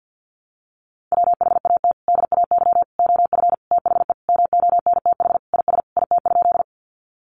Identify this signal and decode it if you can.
Morse code, heard as a keyed tone, about 50 wpm, 700 Hz. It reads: WHAT DAY OF THE MONTH IS IT?
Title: Sentences from Top 800 Words 50wpm